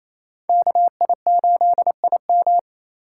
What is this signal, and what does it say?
Morse code, heard as a keyed tone, about 28 wpm, 700 Hz. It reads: KI8IM
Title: USA Call Signs - 5 Characters - 4x Character Spacing - 28wpm